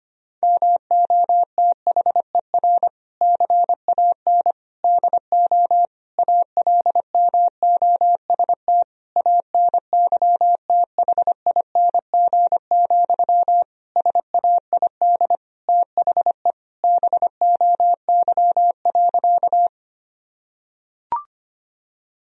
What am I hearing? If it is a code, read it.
Morse code, heard as a keyed tone, about 25 wpm, 700 Hz. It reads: MOTHER CAN DO ALMOST ANYTHING, SAID THE BOY.
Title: Sentences from Top 1000 Words - Review Part 4 - 25wpm